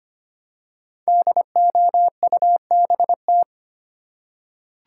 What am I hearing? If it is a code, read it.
Morse code, heard as a keyed tone, about 25 wpm, 700 Hz. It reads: DOUBT
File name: 5 Letter Words - First 3 Letters Given - 25wpm